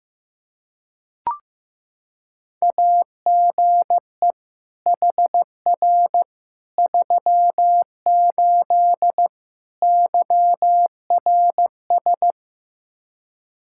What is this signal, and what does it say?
Morse code, heard as a keyed tone, about 15 wpm, 700 Hz. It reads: AGE HR 38 YRS
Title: Practice QSO - Exchange 3 - More Info - 15wpm